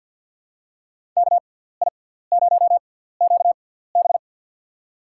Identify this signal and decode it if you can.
Morse code, heard as a keyed tone, about 47 wpm, spaced as for 17 wpm, 700 Hz. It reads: KI0QB
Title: USA Call Signs - 6x Character Spacing - 50wpm